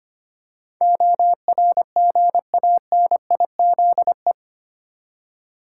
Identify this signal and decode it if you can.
Morse code, heard as a keyed tone, about 25 wpm, 700 Hz. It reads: ORGANIZE
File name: Eight Letter Words - 25wpm